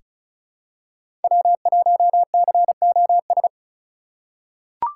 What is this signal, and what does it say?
Morse code, heard as a keyed tone, about 36 wpm, 700 Hz. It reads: W1COS
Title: USA Call Signs - 5 Characters - 4x Character Spacing - 35wpm